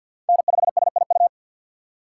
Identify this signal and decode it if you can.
Morse code, heard as a keyed tone, about 52 wpm, 700 Hz. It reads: N5SIU
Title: USA Call Signs - 5 Characters - 3x Character Spacing - 50wpm